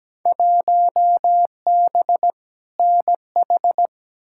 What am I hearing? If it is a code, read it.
Morse code, heard as a keyed tone, about 17 wpm, 700 Hz. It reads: 1B NH
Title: Field Day Exchanges Encore - 17wpm